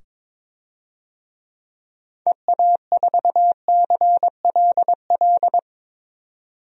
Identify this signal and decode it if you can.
Morse code, heard as a keyed tone, about 21 wpm, 700 Hz. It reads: EA4CLL
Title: DXCC-100 Call Signs - 6 Characters - 22wpm